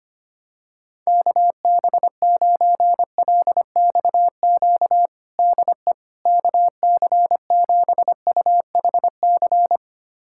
Morse code, heard as a keyed tone, about 25 wpm, 700 Hz. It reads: KB9LXQ DE KC7UHC
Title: USA Callsign DE Callsign - 6 Characters - 25wpm